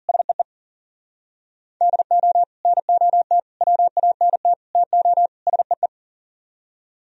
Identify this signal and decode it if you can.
Morse code, heard as a keyed tone, about 52 wpm, 700 Hz. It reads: SEE DO NOT WANT TO SEE